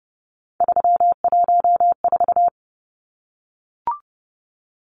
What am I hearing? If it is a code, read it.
Morse code, heard as a keyed tone, about 30 wpm, 700 Hz. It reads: 314
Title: Numbers - 3-Digit 30wpm